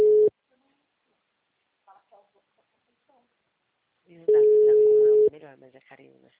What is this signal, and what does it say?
Ringback tone (Europe, 425 Hz), with speech.
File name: out- -2009- - - .59823